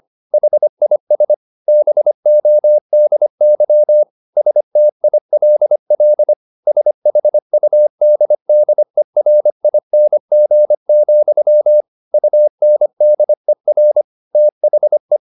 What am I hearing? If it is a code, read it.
Morse code, heard as a keyed tone, about 25 wpm, 600 Hz. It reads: HIS BODY STILL SHUDDERING, UNDER THE
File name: the old man and the sea part 1 25wpm